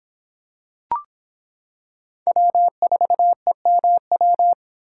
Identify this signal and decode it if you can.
Morse code, heard as a keyed tone, about 26 wpm, 700 Hz. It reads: W4EMW